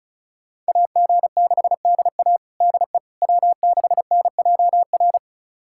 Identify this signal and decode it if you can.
Morse code, heard as a keyed tone, about 34 wpm, 700 Hz. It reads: AG6DA DE W6NJR
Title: USA Callsign DE Callsign - 5 Characters - 35wpm